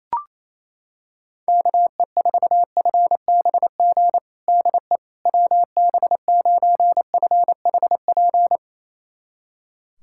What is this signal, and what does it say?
Morse code, heard as a keyed tone, about 28 wpm, 700 Hz. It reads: KE4FBG DE WB9FHP
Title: USA Callsign DE Callsign - 6 Characters - 28wpm